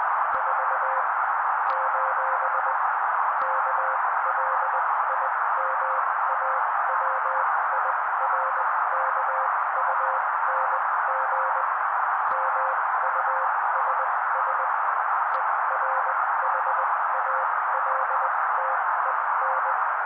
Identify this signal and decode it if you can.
Morse code, heard as a keyed tone, about 20 wpm, spaced as for 15 wpm, 530 Hz. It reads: 4: KLIMAWIRKUNG MUSS ERHALTEN
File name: daily / 16wpm noise